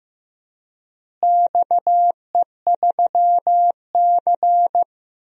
Morse code, heard as a keyed tone, about 16 wpm, 700 Hz. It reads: XE3C
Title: DXCC-100 Call Signs - 4 Characters - Repeated 3x - 15wpm